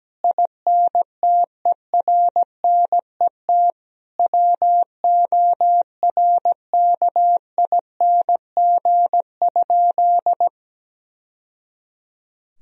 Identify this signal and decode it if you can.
Morse code, heard as a keyed tone, about 17 wpm, 700 Hz. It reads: INTERNET WORKING?
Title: Sentences from Top 1000 Words - Review Part 4 - 17wpm